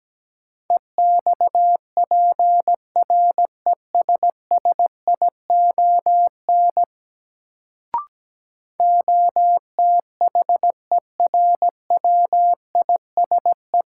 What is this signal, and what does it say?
Morse code, heard as a keyed tone, about 17 wpm, 700 Hz. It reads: EXPRESSION OTHERWISE
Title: Sentences from Top 1200 Words - New Word Drill - 17wpm